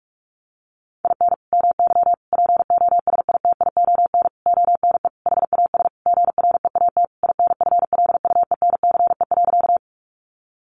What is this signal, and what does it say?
Morse code, heard as a keyed tone, about 44 wpm, 700 Hz. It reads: IN MY POSITION ONE HAS GREAT INFLUENCE.
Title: Sentences from Top 1100 Words 45wpm